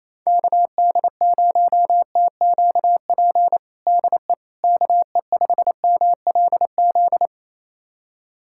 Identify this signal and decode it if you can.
Morse code, heard as a keyed tone, about 28 wpm, 700 Hz. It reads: KD0TQP DE KE5MLZ
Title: USA Callsign DE Callsign - 6 Characters - 28wpm